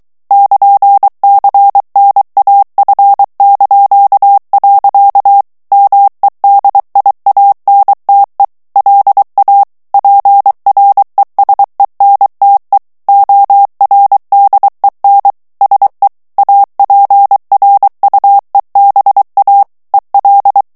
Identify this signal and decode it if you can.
Morse code, heard as a keyed tone, about 23 wpm, 800 Hz. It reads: <KN>CNAF). MEDIANTE LA PRESENTE ORDEN SE APRUEBA EL
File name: morse mp3 / ESPAGNOL TEXTE EN CLAIR 2 VITESSE 22 MOTS MINUTES